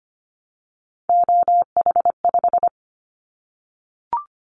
Morse code, heard as a keyed tone, about 25 wpm, 700 Hz. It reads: OH5
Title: Callsign Prefixes for Top 25 DXCC - 25wpm